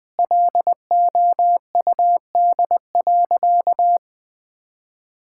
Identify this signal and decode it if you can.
Morse code, heard as a keyed tone, about 21 wpm, 700 Hz. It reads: LOUD.
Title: Sentences with 2-Word Phrases - 20wpm